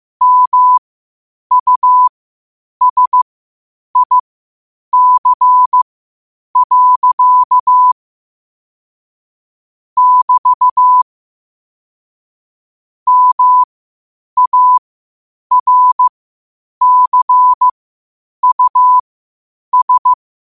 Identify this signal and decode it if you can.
Morse code, heard as a keyed tone, about 15 wpm, spaced as for 8 wpm, 1000 Hz. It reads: MUSIC. <BT> MARCUS
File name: qotdfiles / 8wpm